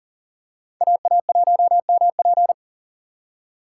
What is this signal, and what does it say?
Morse code, heard as a keyed tone, about 40 wpm, 700 Hz. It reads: AA1MP
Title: USA Call Signs - 5 Characters - 6x Character Spacing - 40wpm